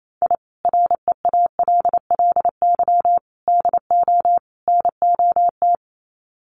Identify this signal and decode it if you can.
Morse code, heard as a keyed tone, about 28 wpm, 700 Hz. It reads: I REALLY DO NOT
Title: Sets of 4 Words 28wpm